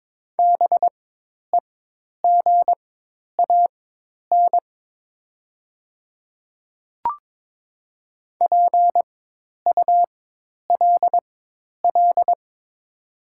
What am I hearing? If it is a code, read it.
Morse code, heard as a keyed tone, about 22 wpm, spaced as for 10 wpm, 700 Hz. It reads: BEGAN PULL
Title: 4x Character Spacing - Top 500 Words - 22wpm